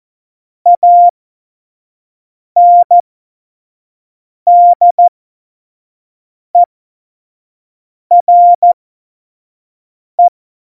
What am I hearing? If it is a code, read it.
Morse code, heard as a keyed tone, about 14 wpm, spaced as for 5 wpm, 700 Hz. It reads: ANDERE
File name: Morsekurs / Witze / Witz 31.Juli 2018, 25 F70